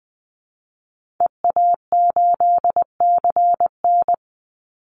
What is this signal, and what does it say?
Morse code, heard as a keyed tone, about 20 wpm, 700 Hz. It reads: EA8CN
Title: Call Signs Repeated 2x - 20wpm